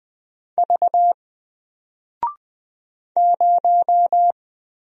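Morse code, heard as a keyed tone, about 20 wpm, 700 Hz. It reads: V 0
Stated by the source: Single letter and number Morse drill at 20 wpm, from N4LCD.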